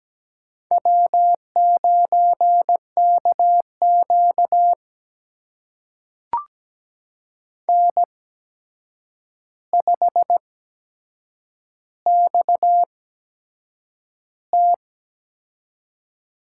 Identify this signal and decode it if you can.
Morse code, heard as a keyed tone, about 17 wpm, spaced as for 7 wpm, 700 Hz. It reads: W9KQ N5XT
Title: USA Call Signs - 8x Character Spacing - 17wpm